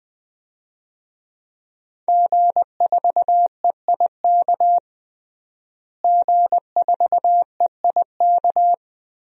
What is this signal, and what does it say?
Morse code, heard as a keyed tone, about 20 wpm, 700 Hz. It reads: G4EIK G4EIK
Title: DXCC-100 Call Signs - 5 Characters - Repeated 2x - 20wpm